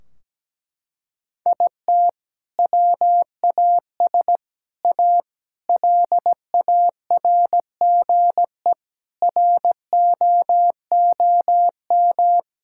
Morse code, heard as a keyed tone, about 17 wpm, 700 Hz. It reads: IT WAS A LARGE ROOM
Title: Sentences from Top 1600 Words 17wpm